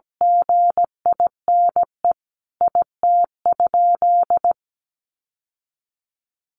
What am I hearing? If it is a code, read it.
Morse code, heard as a keyed tone, about 17 wpm, 700 Hz. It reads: GINE IT?
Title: Sentences from Top 1400 Words 17wpm